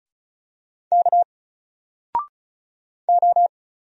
Morse code, heard as a keyed tone, about 36 wpm, 700 Hz. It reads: K O
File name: Single-Letter-Number-35wpm